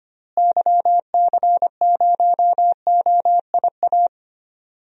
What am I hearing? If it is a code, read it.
Morse code, heard as a keyed tone, about 25 wpm, 700 Hz. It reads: YC0OIA
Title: DXCC-100 Call Signs - 6 Characters - Repeated 2x - 25wpm